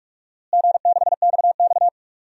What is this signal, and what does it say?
Morse code, heard as a keyed tone, about 44 wpm, 700 Hz. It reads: G6XX
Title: DXCC-100 Call Signs - 4 Characters - Repeated 2x - 45wpm